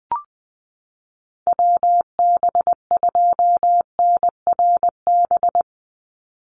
Morse code, heard as a keyed tone, about 20 wpm, 700 Hz. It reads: WB2NRB
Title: USA Call Signs 20wpm